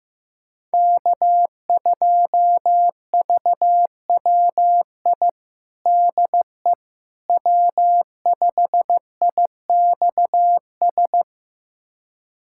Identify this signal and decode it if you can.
Morse code, heard as a keyed tone, about 15 wpm, 700 Hz. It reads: K2VWI DE W5IXS
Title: USA Callsign DE Callsign - 5 Characters - 15wpm